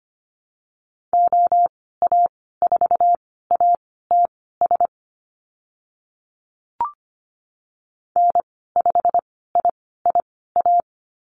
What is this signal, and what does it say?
Morse code, heard as a keyed tone, about 25 wpm, spaced as for 16 wpm, 700 Hz. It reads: OA4ATS N5IIA
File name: Lesson 15g - Number 5 - Call Signs 25wpm